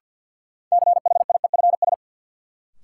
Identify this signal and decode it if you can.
Morse code, heard as a keyed tone, about 51 wpm, 700 Hz. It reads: CHIEFS
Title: Sets of 5 Words Encore - New Word Drill - 50wpm